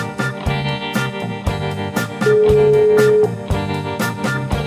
Ringback tone (Europe, 425 Hz), with noise.